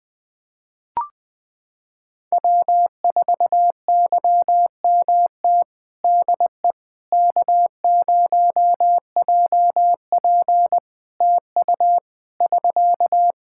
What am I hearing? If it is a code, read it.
Morse code, heard as a keyed tone, about 20 wpm, 700 Hz. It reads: W4YMT DE K0JP TU <SK>
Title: Practice QSO - Exchange 4 - Ending - 20wpm